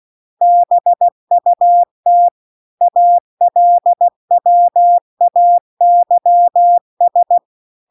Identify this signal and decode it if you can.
Morse code, heard as a keyed tone, about 16 wpm, 700 Hz. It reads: BUT ALWAYS
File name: CW-The-Big-Twoheaded-River-2-16WPM